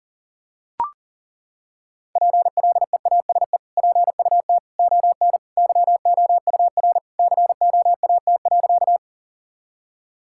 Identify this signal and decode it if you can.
Morse code, heard as a keyed tone, about 40 wpm, 700 Hz. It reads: PLEASE PUT ON YOUR COAT.
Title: Sentences from Top 1600 Words 40wpm